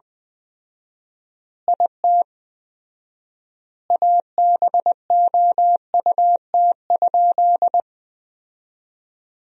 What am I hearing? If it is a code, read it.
Morse code, heard as a keyed tone, about 20 wpm, 700 Hz. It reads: IT ABOUT?
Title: 4x Word-Spacing - Sentences from Top 100 Words - 20wpm